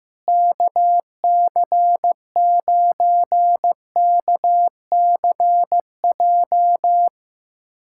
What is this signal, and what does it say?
Morse code, heard as a keyed tone, about 15 wpm, 700 Hz. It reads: KC9KCJ
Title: USA Callsign DE Callsign 15wpm